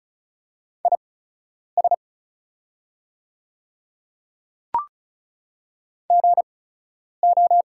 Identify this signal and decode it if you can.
Morse code, heard as a keyed tone, about 36 wpm, spaced as for 10 wpm, 690 Hz. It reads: IS GO
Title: 8x Character Spacing - Top 100 Words - 35wpm